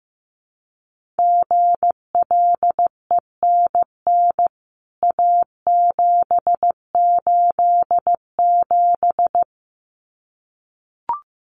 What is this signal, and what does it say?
Morse code, heard as a keyed tone, about 15 wpm, 700 Hz. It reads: GLENN A787
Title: CWT Exchange 15wpm